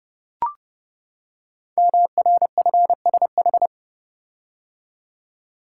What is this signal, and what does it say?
Morse code, heard as a keyed tone, about 30 wpm, 700 Hz. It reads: MRFSH
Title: Code Groups - 5 Letters - 30wpm